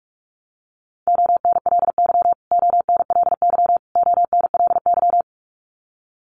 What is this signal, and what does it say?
Morse code, heard as a keyed tone, about 44 wpm, 700 Hz. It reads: ONLY ONLY ONLY
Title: Top 500 Words Repeated 45wpm